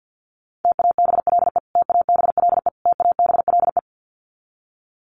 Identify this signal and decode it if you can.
Morse code, heard as a keyed tone, about 50 wpm, 700 Hz. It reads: TABLE TABLE TABLE
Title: Top 500 Words Repeated 50wpm